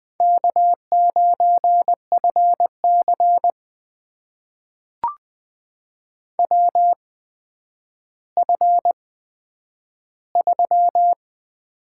Morse code, heard as a keyed tone, about 20 wpm, spaced as for 10 wpm, 700 Hz. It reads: K9FC WF3
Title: USA Call Signs - 4 Characters - 8x Character Spacing - 20wpm